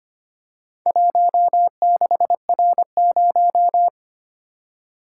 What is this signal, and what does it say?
Morse code, heard as a keyed tone, about 25 wpm, 700 Hz. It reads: 16R0
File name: practice / 25wpm / Four Letter-Number Combinations - 25wpm